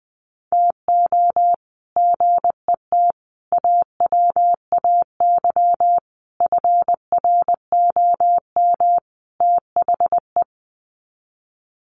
Morse code, heard as a keyed tone, about 20 wpm, 700 Hz. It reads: TO GET AWAY FROM THE
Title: Sets of 5 Words Encore2 20wpm